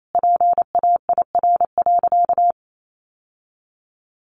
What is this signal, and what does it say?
Morse code, heard as a keyed tone, about 28 wpm, 700 Hz. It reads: PAIR.
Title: Sentences from Top 1200 Words 28wpm